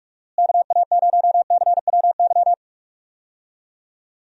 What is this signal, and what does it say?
Morse code, heard as a keyed tone, about 46 wpm, 700 Hz. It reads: KA0CWY
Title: USA Call Signs - 6 Characters - Repeated 3x - 45wpm